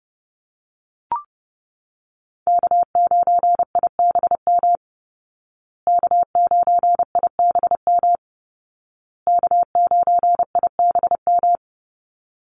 Morse code, heard as a keyed tone, about 30 wpm, 700 Hz. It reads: K9IBM K9IBM K9IBM
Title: Call Signs Repeated 2x - 30wpm